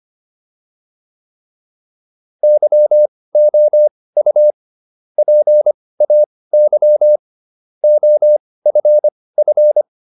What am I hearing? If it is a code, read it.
Morse code, heard as a keyed tone, about 25 wpm, spaced as for 18 wpm, 600 Hz. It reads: YOU PAY OFF
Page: intermediate-practice-files